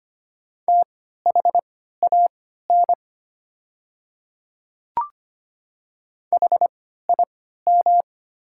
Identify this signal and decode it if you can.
Morse code, heard as a keyed tone, about 25 wpm, spaced as for 14 wpm, 700 Hz. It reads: THAN HIM
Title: 3x Character Spacing - Top 100 Words - 25wpm